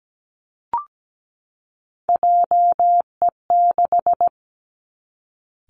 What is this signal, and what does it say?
Morse code, heard as a keyed tone, about 17 wpm, 700 Hz. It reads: JE6
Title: Three Challenging Letter-Number Combinations 17wpm